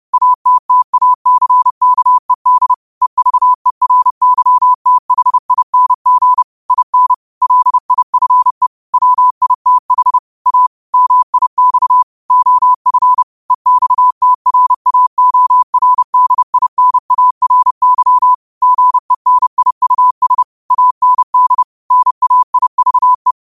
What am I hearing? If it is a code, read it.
Morse code, heard as a keyed tone, about 30 wpm, 1000 Hz. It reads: ATTACKED EVERYTHING IN LIFE WITH A MIX OF EXTRAORDINARY GENIUS AND NAIVE